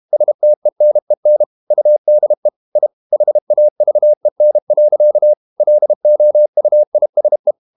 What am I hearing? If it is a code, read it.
Morse code, heard as a keyed tone, about 32 wpm, 600 Hz. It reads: STENEN UDE I HAVEN. LOUISE
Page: Ferie 32 wpm